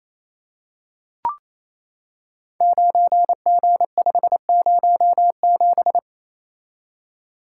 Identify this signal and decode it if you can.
Morse code, heard as a keyed tone, about 28 wpm, 700 Hz. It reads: 9G507
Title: Code Groups - 5 Letter:Numbers - 28wpm